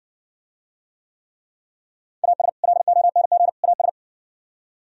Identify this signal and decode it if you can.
Morse code, heard as a keyed tone, about 58 wpm, 700 Hz. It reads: AS LONG AS